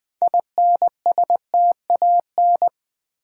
Morse code, heard as a keyed tone, about 20 wpm, 700 Hz. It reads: INSTAN
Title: Build A Word - 3 Letters At A Time - 9-Letter Words - 20wpm